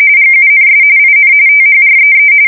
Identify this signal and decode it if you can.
RTTY, about 45.45 baud. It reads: GUNS IN THREE
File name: fsk170 / USS-Missouri